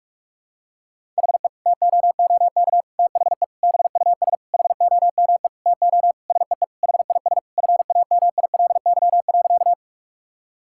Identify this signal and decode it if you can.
Morse code, heard as a keyed tone, about 46 wpm, 700 Hz. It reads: HE TOOK THE BUS HOME TO SEE HIS FAMILY.
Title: Sentences from Top 1000 Words - Review Part 6 - 45wpm